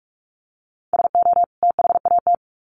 Morse code, heard as a keyed tone, about 44 wpm, 700 Hz. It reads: SO THAT
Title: Sets of 2 Words 45wpm